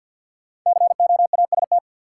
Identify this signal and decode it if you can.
Morse code, heard as a keyed tone, about 52 wpm, 700 Hz. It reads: COAST
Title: Sentences from Top 1000 Words - Review Part 4 - New Word Drill - 50wpm